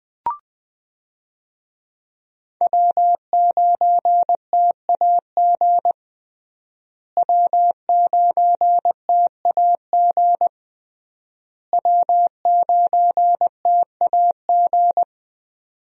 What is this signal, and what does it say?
Morse code, heard as a keyed tone, about 20 wpm, 700 Hz. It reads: W9TAG W9TAG W9TAG
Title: USA Call Signs - 5 Characters - Repeated 3x - 20wpm